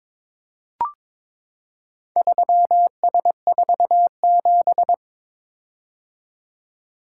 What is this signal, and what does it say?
Morse code, heard as a keyed tone, about 22 wpm, 700 Hz. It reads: 3S47
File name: practice / 22wpm / Four Letter-Number Combinations - 22wpm